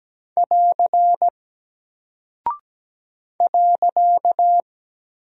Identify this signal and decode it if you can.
Morse code, heard as a keyed tone, about 17 wpm, 700 Hz. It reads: <AR> .
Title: Single Character (Letter:Number:Punctuation:Prosign) Alt 17wpm